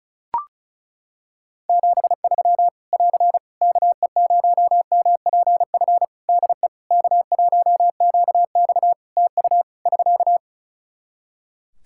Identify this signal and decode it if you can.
Morse code, heard as a keyed tone, about 34 wpm, 700 Hz. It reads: 73 <AR> KE0MPF DE K1QX TU <SK>
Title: Practice QSO - Exchange 4 - Ending - 35wpm